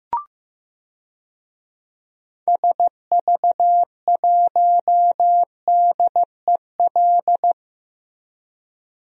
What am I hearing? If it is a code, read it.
Morse code, heard as a keyed tone, about 15 wpm, 700 Hz. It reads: SV1DEL
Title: DXCC-100 Call Signs - 6 Characters - Repeated 2x - 15wpm